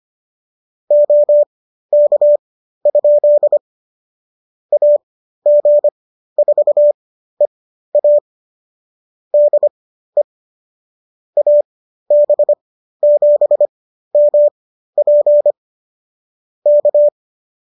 Morse code, heard as a keyed tone, about 25 wpm, spaced as for 13 wpm, 590 Hz. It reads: OK? AG4EA DE AB7MP K